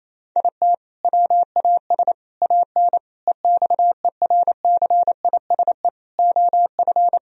Morse code, heard as a keyed tone, about 28 wpm, 700 Hz. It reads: IT WAS AN EXERCISE OF
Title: Sentences from Top 1000 Words - Review Part 4 - 28wpm